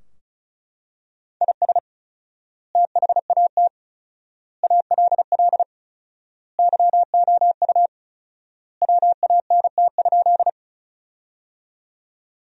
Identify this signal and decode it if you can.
Morse code, heard as a keyed tone, about 36 wpm, 700 Hz. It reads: IS THAT ALL YOU WANT?
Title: 4x Word-Spacing - Sentences from Top 200 Words - 35wpm